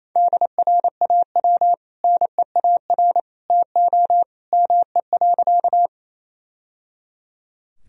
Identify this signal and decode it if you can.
Morse code, heard as a keyed tone, about 28 wpm, 700 Hz. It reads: DRAW NEAR TO ME.
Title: Sentences from Top 1000 Words - Review Part 1 - 28wpm